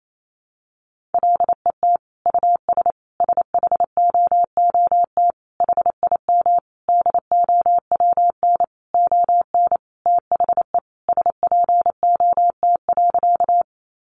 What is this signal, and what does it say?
Morse code, heard as a keyed tone, about 28 wpm, 700 Hz. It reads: LET US SHOOT HIM DOWN ON THE SPOT.
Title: Sentences from Top 1200 Words 28wpm